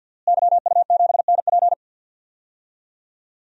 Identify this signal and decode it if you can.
Morse code, heard as a keyed tone, about 48 wpm, 700 Hz. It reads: YU7NP
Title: DXCC-100 Call Signs - 5 Characters - Repeated 2x - 50wpm